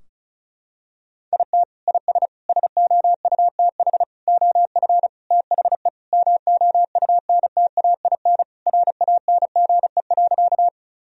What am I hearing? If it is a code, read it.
Morse code, heard as a keyed tone, about 34 wpm, 700 Hz. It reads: IT IS SOUTH OF THE MOUNTAIN RANGE.
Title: Sentences from Top 1000 Words - Review Part 4 - 35wpm